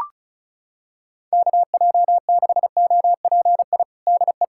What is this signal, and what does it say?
Morse code, heard as a keyed tone, about 34 wpm, 700 Hz. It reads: KJ6OPI DE
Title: USA Callsign DE Callsign - 6 Characters - 35wpm